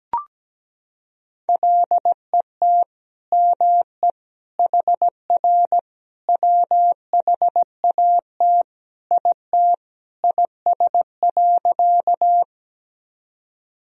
Morse code, heard as a keyed tone, about 17 wpm, 700 Hz. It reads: LET ME HR WHAT IT IS.
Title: Sentences from Top 300 Words and Abbreviations 17wpm